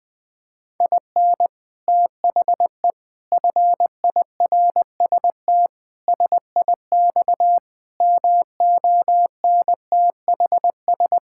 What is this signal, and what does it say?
Morse code, heard as a keyed tone, about 20 wpm, 700 Hz. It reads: IN THE FIRST SIX MONTHS
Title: Sets of 8 Words - 20wpm